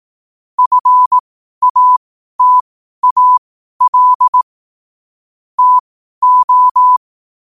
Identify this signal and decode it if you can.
Morse code, heard as a keyed tone, about 18 wpm, spaced as for 12 wpm, 1000 Hz. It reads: FATAL TO
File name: qotdfiles / 12wpm